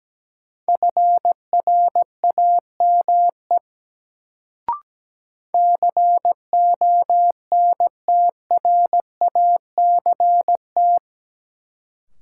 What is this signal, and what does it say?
Morse code, heard as a keyed tone, about 17 wpm, 700 Hz. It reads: FRAME CONTRACT
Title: Sentences from Top 1400 Words - New Word Drill Alt - 17wpm